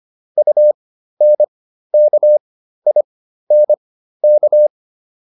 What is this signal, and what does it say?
Morse code, heard as a keyed tone, about 25 wpm, spaced as for 13 wpm, 600 Hz. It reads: UNKINK